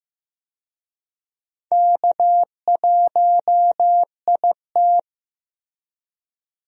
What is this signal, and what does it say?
Morse code, heard as a keyed tone, about 15 wpm, 700 Hz. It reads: K1IT
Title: USA Call Signs - 4 Characters - Repeated 2x - 15wpm